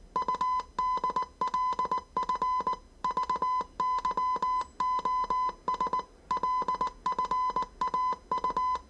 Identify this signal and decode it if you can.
Morse code, heard as a keyed tone, about 19 wpm, 1010 Hz. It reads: UDLF VYOS LFAU
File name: 业余无线电 / ●未分类 / 字码80